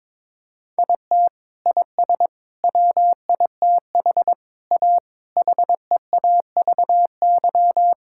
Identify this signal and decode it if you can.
Morse code, heard as a keyed tone, about 22 wpm, 700 Hz. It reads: IT IS WITH A HEAVY
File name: practice / 22wpm / Sentences with 8-Word Phrases - 22wpm